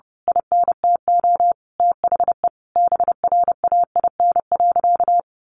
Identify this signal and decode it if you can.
Morse code, heard as a keyed tone, about 30 wpm, 700 Hz. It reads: INTO THE BRAIN.
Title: A Sound Of Thunder 30wpm